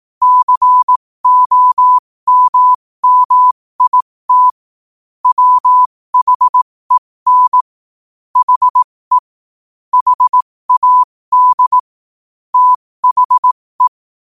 Morse code, heard as a keyed tone, about 18 wpm, spaced as for 15 wpm, 1000 Hz. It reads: COMMIT WHEN HE HAD THE
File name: qotdfiles / 15wpm